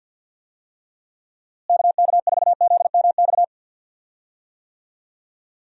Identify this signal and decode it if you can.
Morse code, heard as a keyed tone, about 52 wpm, 700 Hz. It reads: KK4ZMX
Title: USA Call Signs - 6 Characters - Repeated 2x - 50wpm